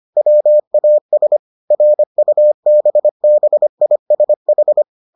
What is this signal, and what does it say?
Morse code, heard as a keyed tone, about 25 wpm, 600 Hz. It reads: WAS RUBBISH